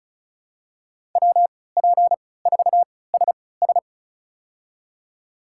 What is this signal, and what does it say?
Morse code, heard as a keyed tone, about 36 wpm, spaced as for 20 wpm, 700 Hz. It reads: WP4SS
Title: USA Call Signs - 5 Characters - 3x Character Spacing - 35wpm